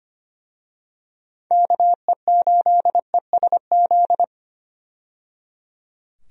Morse code, heard as a keyed tone, about 25 wpm, 700 Hz. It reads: KE8ESZ
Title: USA Call Signs - 6 Characters - 25wpm